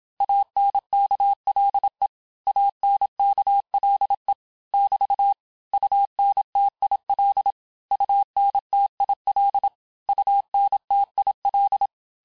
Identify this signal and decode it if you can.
Morse code, heard as a keyed tone, about 27 wpm, 800 Hz. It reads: ANKLE ANKLE <BT> UNTIL UNTIL UNTIL
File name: cours-du-mercredi / audios / cours-869-texte-LES-MOTS-vitesse-25-MPM